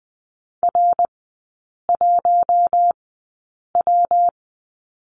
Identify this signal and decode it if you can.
Morse code, heard as a keyed tone, about 20 wpm, spaced as for 8 wpm, 700 Hz. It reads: R1W
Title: 218 Lesson 35e - Number 0 - Three Characters 20wpm